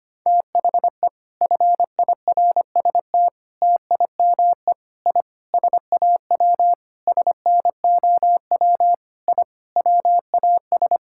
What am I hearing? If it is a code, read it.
Morse code, heard as a keyed tone, about 25 wpm, 700 Hz. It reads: THE FIRST TIME I SAW SNOW I WAS
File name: practice / 25wpm / Sentences with 4-Word Phrases - 25wpm